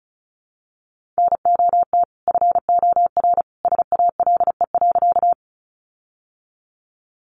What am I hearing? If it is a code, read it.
Morse code, heard as a keyed tone, about 34 wpm, 700 Hz. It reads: NOT FOR SALE.
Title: Sentences from Top 1400 Words 35wpm